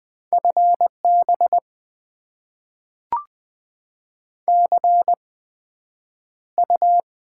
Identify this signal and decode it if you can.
Morse code, heard as a keyed tone, about 20 wpm, spaced as for 9 wpm, 700 Hz. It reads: FB CU
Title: Top 20 QSO Elements - 8x Character Spacing - 20wpm